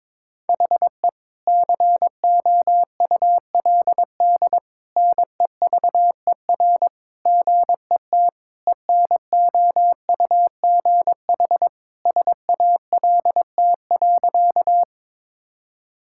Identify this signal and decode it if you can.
Morse code, heard as a keyed tone, about 22 wpm, 700 Hz. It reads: HE COULD NEVER GET ENOUGH SALT.
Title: Sentences from Top 1000 Words - Review Part 5 - 22wpm